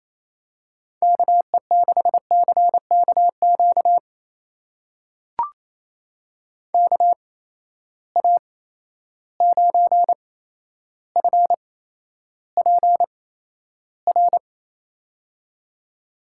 Morse code, heard as a keyed tone, about 27 wpm, spaced as for 12 wpm, 700 Hz. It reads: KE6CKQ KA9FPR
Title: USA Call Signs - 6 Characters - 8x Character Spacing - 28wpm